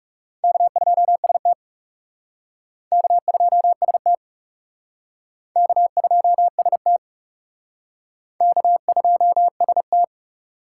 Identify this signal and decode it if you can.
Morse code, heard as a keyed tone, about 37 wpm, 700 Hz. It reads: K2ST K2ST K2ST K2ST
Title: Speed Racing - USA Call Signs 30wpm